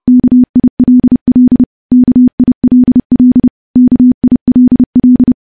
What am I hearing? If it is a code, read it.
Morse code, heard as a keyed tone, about 30 wpm, 250 Hz. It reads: KILL KILL KILL
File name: morse / morse-training-30WPM